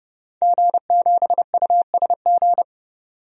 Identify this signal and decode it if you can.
Morse code, heard as a keyed tone, about 30 wpm, 700 Hz. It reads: G7USG
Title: DXCC-100 Call Signs - 5 Characters - 30wpm